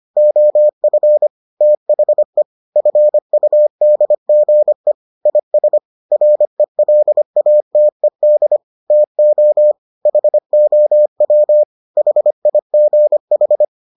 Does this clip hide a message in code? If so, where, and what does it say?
Yes: Morse code from 0.2 to 13.7 s, heard as a keyed tone, about 25 wpm, 600 Hz. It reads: OF THE FUDGE IS RELATED TO HOW HIGH